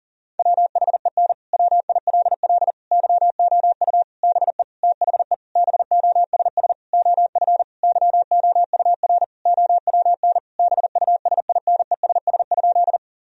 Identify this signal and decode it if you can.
Morse code, heard as a keyed tone, about 40 wpm, 700 Hz. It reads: WHEN WILL YOU BE THE BOSS OF YOUR OWN BUSINESS?
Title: Sentences from Top 1000 Words - Review Part 4 - 40wpm